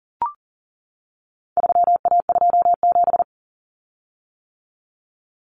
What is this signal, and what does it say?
Morse code, heard as a keyed tone, about 40 wpm, 700 Hz. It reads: 3A27
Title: Common Cut Numbers - 4-Digit 40wpm